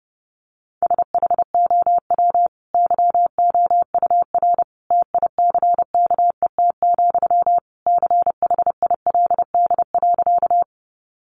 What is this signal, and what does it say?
Morse code, heard as a keyed tone, about 30 wpm, 700 Hz. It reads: SHOW YOUR TICKET, CHILD.
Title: Sentences from Top 1400 Words 30wpm